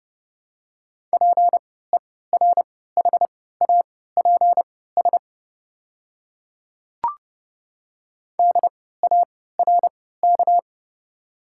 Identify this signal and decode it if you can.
Morse code, heard as a keyed tone, about 30 wpm, spaced as for 17 wpm, 690 Hz. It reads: PERHAPS DARK
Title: 3x Character Spacing - Top 500 Words - 30wpm